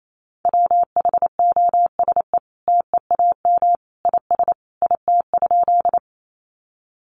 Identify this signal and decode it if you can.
Morse code, heard as a keyed tone, about 28 wpm, 700 Hz. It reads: WHOSE TEAM IS IT?
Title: Sentences from Top 1500 Words 28wpm